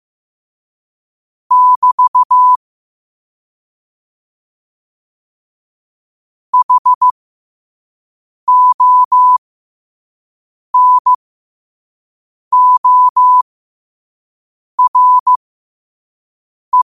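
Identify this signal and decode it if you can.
Morse code, heard as a keyed tone, about 15 wpm, spaced as for 5 wpm, 1000 Hz. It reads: <BT> HONORE
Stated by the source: Morse code at 5 words per minute.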